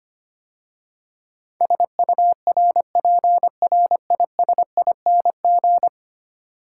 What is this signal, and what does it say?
Morse code, heard as a keyed tone, about 25 wpm, 700 Hz. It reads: SURPRISING
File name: 10 Letter Words - First 3 Letters Given - 25wpm